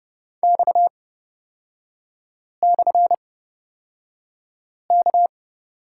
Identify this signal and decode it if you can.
Morse code, heard as a keyed tone, about 30 wpm, spaced as for 5 wpm, 700 Hz. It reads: X/K
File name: Lesson 37e - Letter x - Three Characters 30wpm